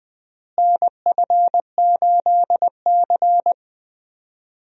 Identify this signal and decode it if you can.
Morse code, heard as a keyed tone, about 20 wpm, 700 Hz. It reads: NF8C
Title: USA Call Signs - 4 Characters - Repeated 3x - 20wpm